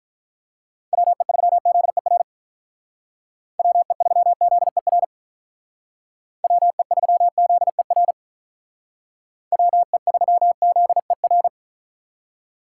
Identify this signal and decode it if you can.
Morse code, heard as a keyed tone, about 46 wpm, 700 Hz. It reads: WE3ZER WE3ZER WE3ZER WE3ZER